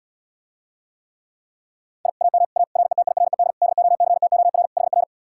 Morse code, heard as a keyed tone, about 79 wpm, 700 Hz. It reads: I AM A RETIRED DOCTOR HR